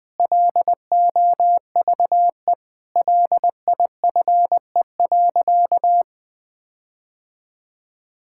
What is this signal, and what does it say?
Morse code, heard as a keyed tone, about 20 wpm, 700 Hz. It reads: LOVE LIFE.
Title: Sentences with 8-Word Phrases - 20wpm